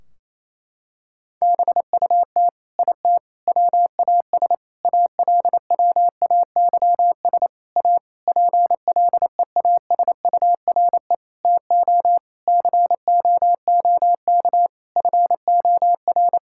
Morse code, heard as a keyed tone, about 28 wpm, 700 Hz. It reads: BUT IT WAS ALWAYS A PLEASURE TO COOK FOR
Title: Sentences from Top 1600 Words 28wpm